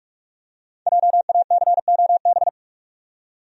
Morse code, heard as a keyed tone, about 46 wpm, 700 Hz. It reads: JACOB
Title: USA Common Names - 45wpm